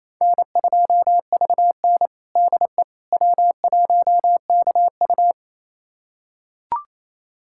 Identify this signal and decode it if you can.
Morse code, heard as a keyed tone, about 28 wpm, 700 Hz. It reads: N2VN DE W1KU
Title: USA Callsign DE Callsign 28wpm